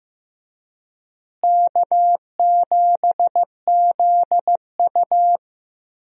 Morse code, heard as a keyed tone, about 15 wpm, 700 Hz. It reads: K7ZU
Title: USA Call Signs - 4 Characters - Repeated 2x - 15wpm